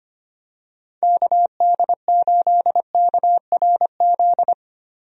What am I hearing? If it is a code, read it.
Morse code, heard as a keyed tone, about 25 wpm, 700 Hz. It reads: KD8KRZ
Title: USA Call Signs - 6 Characters - Repeated 2x - 25wpm